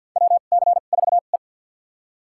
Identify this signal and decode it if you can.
Morse code, heard as a keyed tone, about 47 wpm, spaced as for 35 wpm, 700 Hz. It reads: WC4E
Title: USA Call Signs - 4 Characters - 2x Character Spacing - 50wpm